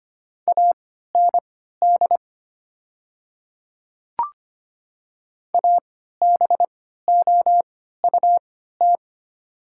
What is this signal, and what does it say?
Morse code, heard as a keyed tone, about 25 wpm, spaced as for 14 wpm, 700 Hz. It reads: AND ABOUT
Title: 3x Character Spacing - Top 500 Words - 25wpm